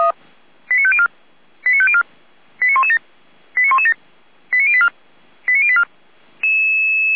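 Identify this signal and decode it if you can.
Five-tone selective call (ZVEI), heard as quick runs of five tones, tones of about 70 ms. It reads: 89504 89504 89117 89117 89084 89084
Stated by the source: Five-tone ZVEI selective call received by SDR.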